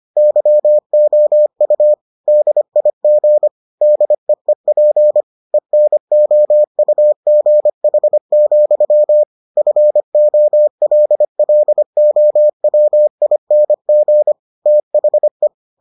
Morse code, heard as a keyed tone, about 25 wpm, 600 Hz. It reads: YOU DIG DEEP ENOUGH, FOLLOWING THE